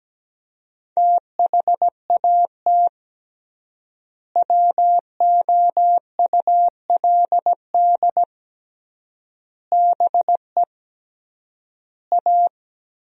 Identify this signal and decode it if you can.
Morse code, heard as a keyed tone, about 17 wpm, 700 Hz. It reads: THAT WOULD BE A
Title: 3x Word-Spacing - Sentences from Top 500 Words - 17wpm